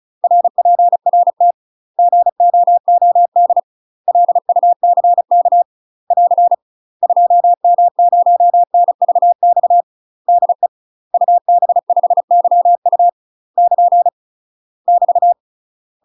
Morse code, heard as a keyed tone, about 34 wpm, 700 Hz. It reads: RPRT GOOD LUCK <AR> 2M0NVX DE UB5YU <KN> <BT>
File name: UK MASTER QSOs TRAINING / 35WPM / UK MASTER QSOs 0008